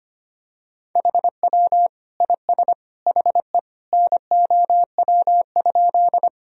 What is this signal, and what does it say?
Morse code, heard as a keyed tone, about 25 wpm, 700 Hz. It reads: HW IS HE NOW?
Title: Sentences from Top 100 Words and Abbreviations 25wpm